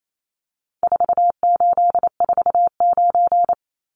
Morse code, heard as a keyed tone, about 28 wpm, 700 Hz. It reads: 4849
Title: Numbers - 4-Digit 28wpm